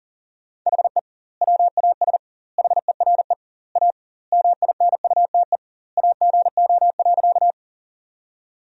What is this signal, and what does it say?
Morse code, heard as a keyed tone, about 40 wpm, 700 Hz. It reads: HE WAS HERE A MINUTE AGO.
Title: 2x Word-Spacing - Sentences from Top 500 Words - 40wpm